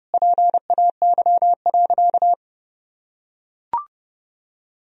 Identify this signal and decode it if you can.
Morse code, heard as a keyed tone, about 32 wpm, 700 Hz. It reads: PAY.
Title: Sentences from Top 1000 Words - Review Part 4 - 30wpm